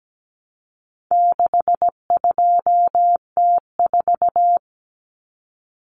Morse code, heard as a keyed tone, about 17 wpm, 700 Hz. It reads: 62T4
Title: Common Cut Numbers - 4-Digit 17wpm